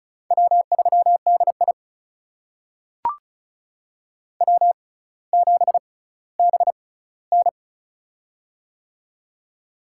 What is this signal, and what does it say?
Morse code, heard as a keyed tone, about 36 wpm, spaced as for 18 wpm, 700 Hz. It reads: W3DI W7BN
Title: USA Call Signs - 4 Characters - 6x Character Spacing - 35wpm